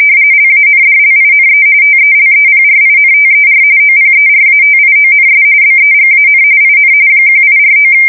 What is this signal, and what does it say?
RTTY, about 45.45 baud. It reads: LANDLINE TELEPRINTER OPERATIONS BEGAN IN 1849
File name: fsk170 / teletype-RTTY